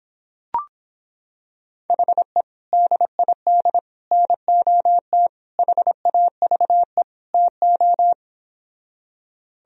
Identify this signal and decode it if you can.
Morse code, heard as a keyed tone, about 26 wpm, 700 Hz. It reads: HE DID NOT HAVE TO